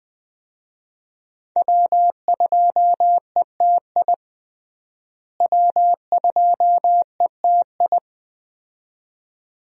Morse code, heard as a keyed tone, about 21 wpm, 700 Hz. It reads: W2ETI W2ETI
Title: USA Call Signs - 5 Characters - Repeated 2x - 20wpm